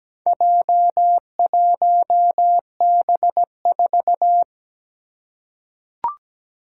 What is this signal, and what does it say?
Morse code, heard as a keyed tone, about 17 wpm, 700 Hz. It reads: J1B4
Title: Four Challenging Letter-Number Combinations 17wpm